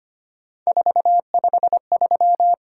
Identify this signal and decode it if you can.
Morse code, heard as a keyed tone, about 25 wpm, 700 Hz. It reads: 453
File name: Three Letter-Number Combinations - 25wpm